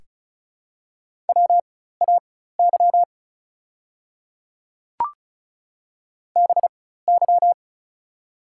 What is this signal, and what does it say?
Morse code, heard as a keyed tone, about 36 wpm, spaced as for 16 wpm, 700 Hz. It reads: WAY BY
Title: 4x Character Spacing - Top 100 Words - 35wpm